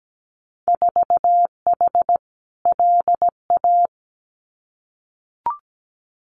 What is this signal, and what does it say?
Morse code, heard as a keyed tone, about 17 wpm, 700 Hz. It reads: 4H LA
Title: Winter Field Day Exchanges Encore 17wpm